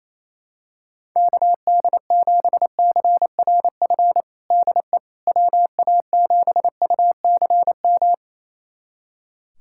Morse code, heard as a keyed tone, about 28 wpm, 700 Hz. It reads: KD7CRF DE WA7UCM
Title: USA Callsign DE Callsign - 6 Characters - 28wpm